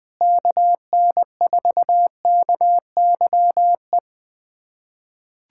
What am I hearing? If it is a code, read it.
Morse code, heard as a keyed tone, about 21 wpm, 700 Hz. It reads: KN4KYE
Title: USA Callsign DE Callsign 20wpm